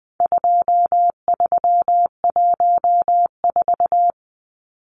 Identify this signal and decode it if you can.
Morse code, heard as a keyed tone, about 20 wpm, 700 Hz. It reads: 2314
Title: Numbers - 4-Digit 20wpm